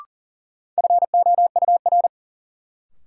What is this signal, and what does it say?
Morse code, heard as a keyed tone, about 40 wpm, 700 Hz. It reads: FOUR
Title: Sentences from Top 300 Words - New Word Drill - 40wpm